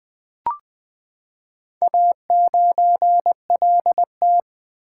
Morse code, heard as a keyed tone, about 20 wpm, 700 Hz. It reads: A9LT
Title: Four Letter-Number Combinations - 20wpm